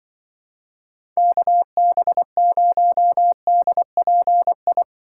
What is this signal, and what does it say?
Morse code, heard as a keyed tone, about 24 wpm, 700 Hz. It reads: KB0DPI
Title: Speed Racing - USA 2x3 Call Signs 20wpm